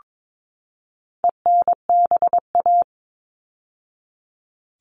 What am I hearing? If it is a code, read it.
Morse code, heard as a keyed tone, about 22 wpm, 700 Hz. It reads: ENBA
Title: All Cut Numbers - 4-Digit 22wpm